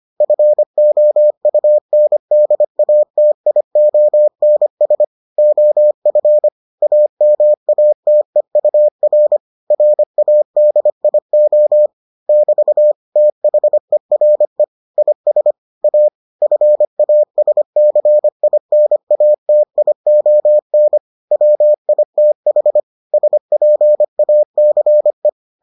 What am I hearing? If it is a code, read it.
Morse code, heard as a keyed tone, about 25 wpm, 600 Hz. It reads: FOUNDATIONS OF AMATEUR RADIO <BT> THERE IS A FASCINATION WITH SPACE